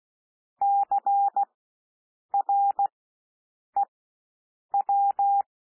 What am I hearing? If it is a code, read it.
Morse code, heard as a keyed tone, about 16 wpm, spaced as for 7 wpm, 800 Hz. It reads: CREW
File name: cwmp3 / W1 SOUND WORDS 16 wpm